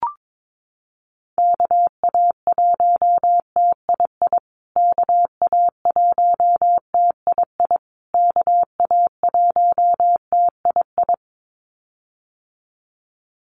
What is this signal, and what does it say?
Morse code, heard as a keyed tone, about 22 wpm, 700 Hz. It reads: KA1TII KA1TII KA1TII
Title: USA Call Signs Repeated 3x 22wpm